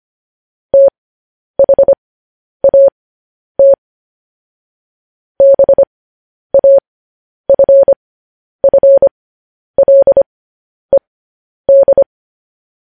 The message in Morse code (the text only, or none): THAT BAFFLED